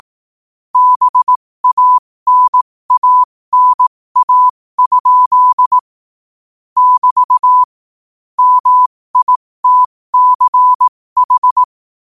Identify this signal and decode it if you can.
Morse code, heard as a keyed tone, about 18 wpm, spaced as for 15 wpm, 1000 Hz. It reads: BANANA? <BT> MITCH
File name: qotdfiles / 15wpm